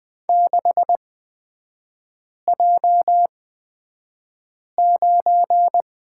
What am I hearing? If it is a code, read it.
Morse code, heard as a keyed tone, about 20 wpm, spaced as for 5 wpm, 700 Hz. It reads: 6J9
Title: 243 Lesson 39e - Letter z - Three Characters 20wpm